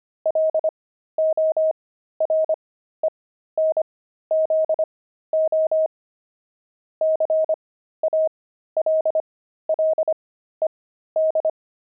Morse code, heard as a keyed tone, about 25 wpm, spaced as for 13 wpm, 630 Hz. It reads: LORENZO CALLED
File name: pangram-fs-13-cs-25-kidding